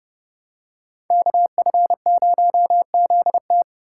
Morse code, heard as a keyed tone, about 28 wpm, 700 Hz. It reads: KF0ZT